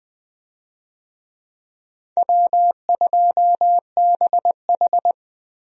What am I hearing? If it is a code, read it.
Morse code, heard as a keyed tone, about 21 wpm, 700 Hz. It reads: W2BH
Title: USA Call Signs - 4 Characters - Repeated 3x - 20wpm